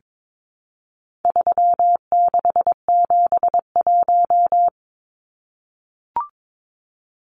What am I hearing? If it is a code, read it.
Morse code, heard as a keyed tone, about 22 wpm, 700 Hz. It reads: 3671
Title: Numbers - 4-Digit 22wpm